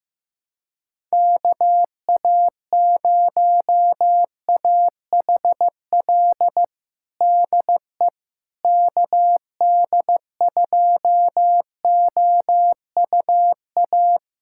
Morse code, heard as a keyed tone, about 15 wpm, 700 Hz. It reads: KA0AHL DE KD2OUA
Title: USA Callsign DE Callsign 15wpm